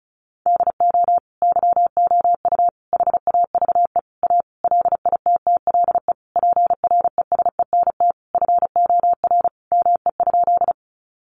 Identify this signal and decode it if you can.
Morse code, heard as a keyed tone, about 34 wpm, 700 Hz. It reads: DO YOU HAVE A LITTLE PRESENT FOR ME?
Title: Sentences from Top 800 Words 35wpm